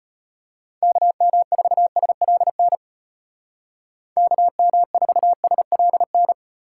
Morse code, heard as a keyed tone, about 33 wpm, 700 Hz. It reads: KM4SLN KM4SLN